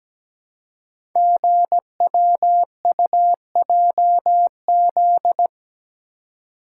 Morse code, heard as a keyed tone, about 17 wpm, 700 Hz. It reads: GWUJZ
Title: Code Groups - 5 Letters - 17wpm